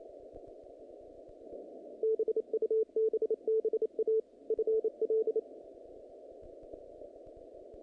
Morse code, heard as a keyed tone, about 27 wpm, 440 Hz. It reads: BUBBA FL
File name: contests / naqp-cw-k5tr-op-k5pi- -2200